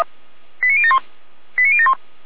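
Five-tone selective call (ZVEI), heard as quick runs of five tones, tones of about 70 ms. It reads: 89071 89071
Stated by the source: ZVEI five-tone selective call 89110, received by SDR.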